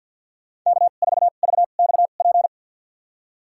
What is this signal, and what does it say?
Morse code, heard as a keyed tone, about 49 wpm, spaced as for 36 wpm, 700 Hz. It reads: K4VXP
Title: USA Call Signs - 5 Characters - 2x Character Spacing - 50wpm